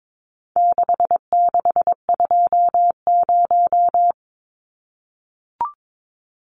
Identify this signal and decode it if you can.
Morse code, heard as a keyed tone, about 22 wpm, 700 Hz. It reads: 6620
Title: Numbers - 4-Digit 22wpm